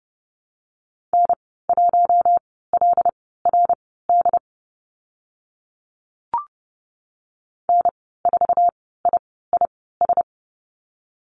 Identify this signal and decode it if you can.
Morse code, heard as a keyed tone, about 30 wpm, spaced as for 17 wpm, 700 Hz. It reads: N1LRD N4IIS
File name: Lesson 14g - Number 2 - Call Signs 30wpm